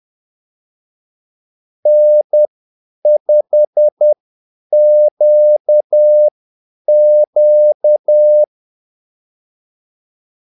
Morse code, heard as a keyed tone, about 10 wpm, spaced as for 8 wpm, 600 Hz. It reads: N5QQ
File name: Callsigns-10-8wpm